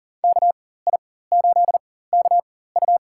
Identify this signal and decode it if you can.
Morse code, heard as a keyed tone, about 40 wpm, spaced as for 19 wpm, 700 Hz. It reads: KI8KU
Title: USA Call Signs - 5 Characters - 4x Character Spacing - 40wpm